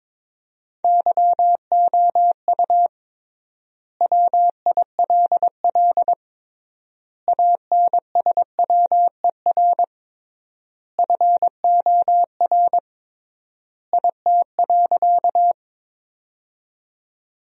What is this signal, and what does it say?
Morse code, heard as a keyed tone, about 22 wpm, 700 Hz. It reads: YOU WILL ANSWER FOR IT.
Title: 3x Word-Spacing - Sentences from Top 300 Words - 22wpm